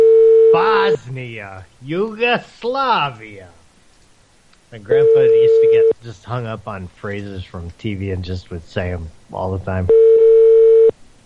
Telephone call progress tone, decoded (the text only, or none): ringback tone